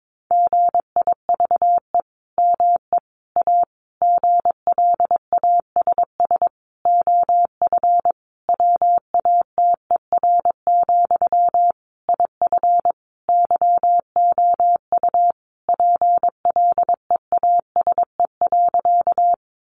Morse code, heard as a keyed tone, about 22 wpm, 700 Hz. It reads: GIVE ME A GLASS OF WATER, IF YOU PLEASE.
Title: Sentences from Top 1500 Words 22wpm